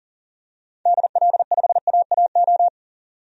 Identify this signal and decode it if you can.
Morse code, heard as a keyed tone, about 40 wpm, 700 Hz. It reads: DL5AAO
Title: DXCC-100 Call Signs - 6 Characters - 40wpm